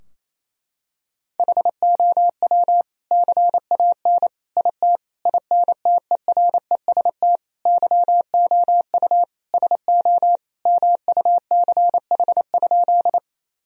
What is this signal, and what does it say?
Morse code, heard as a keyed tone, about 28 wpm, 700 Hz. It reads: HOW CAN IT INTEREST YOU SO MUCH?
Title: Sentences from Top 1000 Words - Review Part 2 - 28wpm